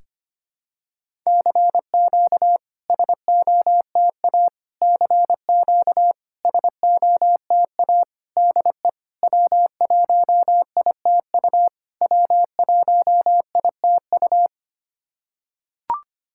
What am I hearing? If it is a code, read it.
Morse code, heard as a keyed tone, about 25 wpm, 700 Hz. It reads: CQ SOTA CQ SOTA DE W1ITU W1ITU
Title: SOTA QSO Practice 25wpm